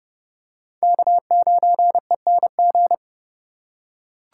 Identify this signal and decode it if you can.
Morse code, heard as a keyed tone, about 27 wpm, 700 Hz. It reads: K9ENG